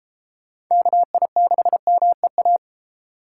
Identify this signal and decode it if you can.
Morse code, heard as a keyed tone, about 36 wpm, 700 Hz. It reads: KI6MEA